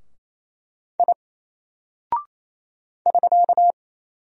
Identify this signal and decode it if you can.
Morse code, heard as a keyed tone, about 27 wpm, 700 Hz. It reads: I <SK>
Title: Single Character (Letter:Number:Punctuation:Prosign) 28wpm